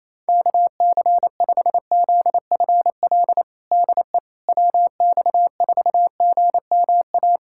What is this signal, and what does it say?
Morse code, heard as a keyed tone, about 28 wpm, 700 Hz. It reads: KC5ZFL DE WX4GMA
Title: USA Callsign DE Callsign - 6 Characters - 28wpm